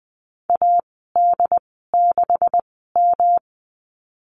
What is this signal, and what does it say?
Morse code, heard as a keyed tone, about 20 wpm, spaced as for 15 wpm, 700 Hz. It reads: AD6M
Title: 147 Lesson 24g - Letter y - Call Signs 20wpm